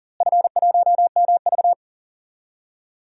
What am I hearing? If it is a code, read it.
Morse code, heard as a keyed tone, about 39 wpm, 690 Hz. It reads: F1MV
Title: DXCC-100 Call Signs - 4 Characters - Repeated 2x - 40wpm